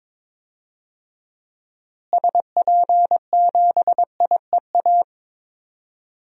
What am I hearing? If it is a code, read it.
Morse code, heard as a keyed tone, about 22 wpm, 700 Hz. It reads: SP7IEA
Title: DXCC-100 Call Signs - 6 Characters - Repeated 3x - 22wpm